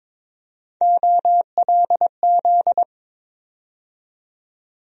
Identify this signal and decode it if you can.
Morse code, heard as a keyed tone, about 22 wpm, 700 Hz. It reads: OLZ
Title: USA Call Sign Postfixes - 3 Letters 22wpm